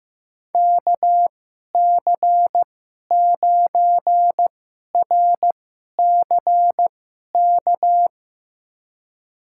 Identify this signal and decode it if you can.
Morse code, heard as a keyed tone, about 15 wpm, spaced as for 11 wpm, 700 Hz. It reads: KC9RCK
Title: USA Call Signs - 6 Characters - 2x Character Spacing - 15wpm